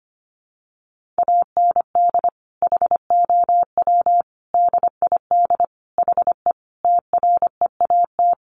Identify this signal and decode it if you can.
Morse code, heard as a keyed tone, about 25 wpm, 700 Hz. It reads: AND HOW DID HE TREAT
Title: Sentences from Top 1300 Words 25wpm